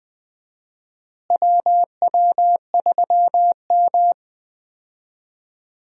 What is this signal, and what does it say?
Morse code, heard as a keyed tone, about 20 wpm, 700 Hz. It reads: WW3M
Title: USA Call Signs - 4 Characters - Repeated 2x - 20wpm